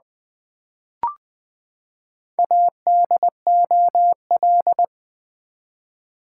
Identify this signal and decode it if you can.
Morse code, heard as a keyed tone, about 20 wpm, 700 Hz. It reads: ADOL
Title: Build A Word - 4 Letters At A Time - 10-Letter Words - 20wpm